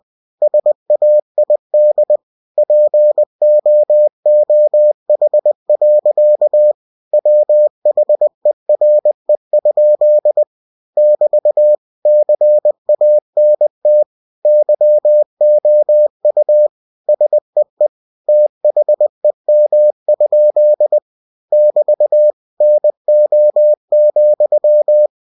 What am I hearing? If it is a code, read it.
Morse code, heard as a keyed tone, about 20 wpm, 600 Hz. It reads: SAID POOH. WHERE? <BT> CANT YOU SEE THEM? <BT> NO,
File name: cw-books / winnie-the-pooh / winnie-20wpm-page-0047